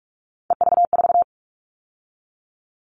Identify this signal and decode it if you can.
Morse code, heard as a keyed tone, about 44 wpm, 700 Hz. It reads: EV4
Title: All Cut Numbers - 3-Digit 45wpm